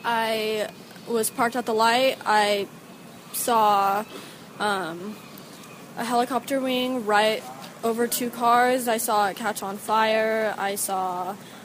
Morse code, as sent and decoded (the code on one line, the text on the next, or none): none